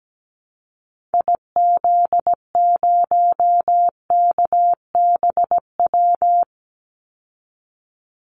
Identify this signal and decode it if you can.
Morse code, heard as a keyed tone, about 17 wpm, 700 Hz. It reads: IZ0KBW
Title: Call Signs Repeated 2x - 17wpm